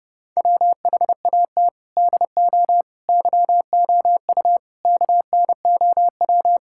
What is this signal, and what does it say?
Morse code, heard as a keyed tone, about 30 wpm, 700 Hz. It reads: WHAT DO YOU KNOW
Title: Sentences from Top 1000 Words - Review Part 6 - 30wpm